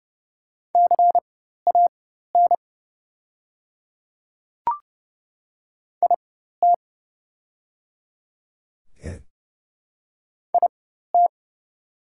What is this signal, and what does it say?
Morse code, heard as a keyed tone, about 30 wpm, spaced as for 14 wpm, 700 Hz. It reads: CAN IT IT
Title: 4x Character Spacing - Top 100 Words - 30wpm